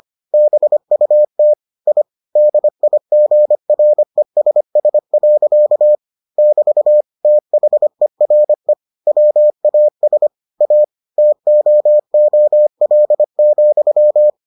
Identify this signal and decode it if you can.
Morse code, heard as a keyed tone, about 25 wpm, 600 Hz. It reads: BUT I DIGRESS. <BT> THERE WAS A TOOL,